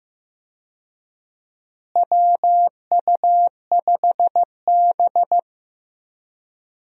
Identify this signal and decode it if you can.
Morse code, heard as a keyed tone, about 15 wpm, 700 Hz. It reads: WU5B
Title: USA Call Signs - 4 Characters - Repeated 2x - 15wpm